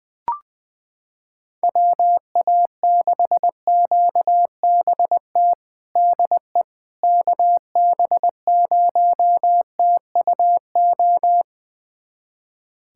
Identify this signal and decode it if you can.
Morse code, heard as a keyed tone, about 21 wpm, 700 Hz. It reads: WA6QBT DE KB0TUO
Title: USA Callsign DE Callsign 20wpm